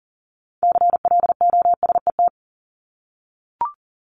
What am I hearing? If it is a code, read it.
Morse code, heard as a keyed tone, about 40 wpm, 700 Hz. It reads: CLOSET
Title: Top 1751-2000 Words 40wpm